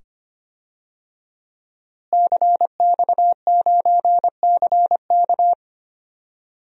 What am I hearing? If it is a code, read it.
Morse code, heard as a keyed tone, about 25 wpm, 700 Hz. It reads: CX9CK
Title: DXCC-100 Call Signs - 5 Characters - Repeated 2x - 25wpm